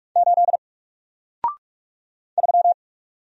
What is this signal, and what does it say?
Morse code, heard as a keyed tone, about 44 wpm, 700 Hz. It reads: 8 3
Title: Single Letter-Number Alt 45wpm